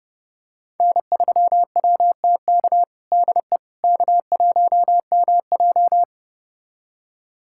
Morse code, heard as a keyed tone, about 30 wpm, 700 Hz. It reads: N3WTK DE K1MJ
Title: USA Callsign DE Callsign 30wpm